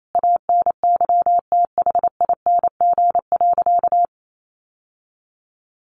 Morse code, heard as a keyed tone, about 28 wpm, 700 Hz. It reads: ANYTHING.
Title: Sentences from Top 1400 Words 28wpm